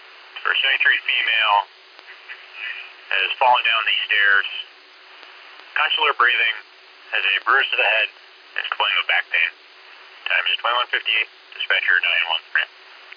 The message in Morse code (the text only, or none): none